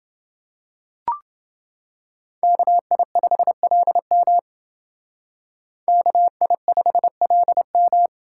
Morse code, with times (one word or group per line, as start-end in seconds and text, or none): 2.4-4.4 KI5LM
5.9-8.1 KI5LM